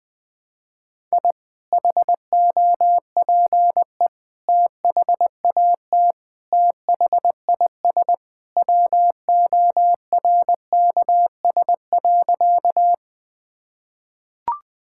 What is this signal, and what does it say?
Morse code, heard as a keyed tone, about 20 wpm, 700 Hz. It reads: I HOPE THAT THIS WORKS.
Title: Sentences with 2-Word Phrases - Spoken Beginning - 20wpm